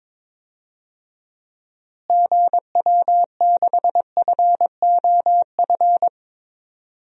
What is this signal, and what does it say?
Morse code, heard as a keyed tone, about 22 wpm, 700 Hz. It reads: GW6FOF
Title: DXCC-100 Call Signs - 6 Characters - Repeated 2x - 22wpm